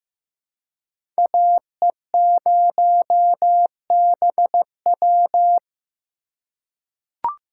Morse code, heard as a keyed tone, about 15 wpm, 700 Hz. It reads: AE0BW
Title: USA Call Signs - 3x Character Spacing - 15wpm